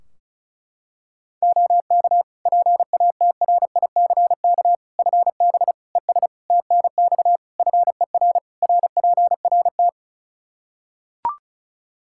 Morse code, heard as a keyed tone, about 34 wpm, 700 Hz. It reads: OK PATRICK FB ES TNX FER RPRT
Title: Practice QSO - Exchange 2 - Info - 35wpm